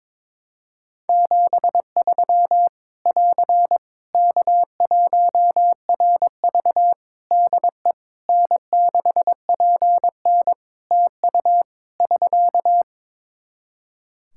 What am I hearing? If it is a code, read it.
Morse code, heard as a keyed tone, about 21 wpm, 700 Hz. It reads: 73 <AR> K1RV DE N6PN TU <SK>
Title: Practice QSO - Exchange 4 - Ending - 22wpm